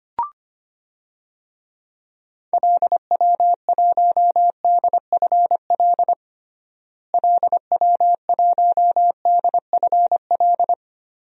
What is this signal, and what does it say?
Morse code, heard as a keyed tone, about 25 wpm, 700 Hz. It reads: LW1DFL LW1DFL
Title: DXCC-100 Call Signs - 6 Characters - Repeated 3x - 25wpm